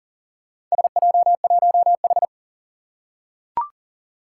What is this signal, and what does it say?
Morse code, heard as a keyed tone, about 40 wpm, 700 Hz. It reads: SJ1H
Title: Four Challenging Letter-Number Combinations 40wpm